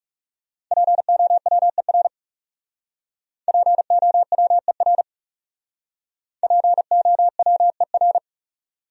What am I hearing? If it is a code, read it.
Morse code, heard as a keyed tone, about 36 wpm, 700 Hz. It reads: POWER POWER POWER
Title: Speed-Racing Top 500 Words 30wpm